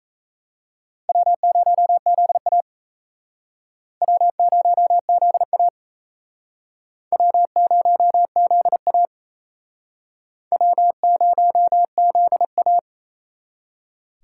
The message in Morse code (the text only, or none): W0ZA W0ZA W0ZA W0ZA